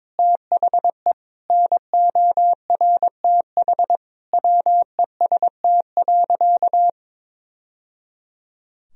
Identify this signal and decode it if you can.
Morse code, heard as a keyed tone, about 22 wpm, 700 Hz. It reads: THE NORTH WEST.
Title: Sentences from Top 1000 Words - Review Part 2 - 22wpm